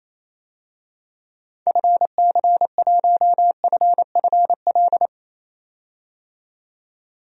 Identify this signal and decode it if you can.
Morse code, heard as a keyed tone, about 28 wpm, 700 Hz. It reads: FC1FFL
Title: DXCC-100 Call Signs - 6 Characters - 28wpm